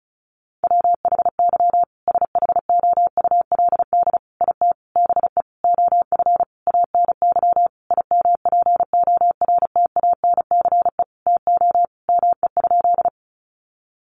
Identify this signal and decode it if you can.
Morse code, heard as a keyed tone, about 34 wpm, 700 Hz. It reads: WHY SHOULD IT BE OF ANY IMPORTANCE TO ME?
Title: Sentences from Top 1200 Words 35wpm